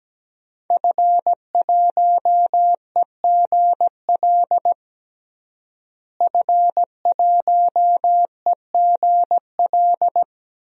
Morse code, heard as a keyed tone, about 17 wpm, 700 Hz. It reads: F1EGL F1EGL
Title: DXCC-100 Call Signs - 5 Characters - Repeated 2x - 17wpm